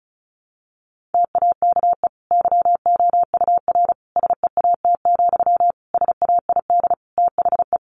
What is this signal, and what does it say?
Morse code, heard as a keyed tone, about 34 wpm, 700 Hz. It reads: TAKE YOUR SEAT, SAID THE
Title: Sentences from Top 1500 Words 35wpm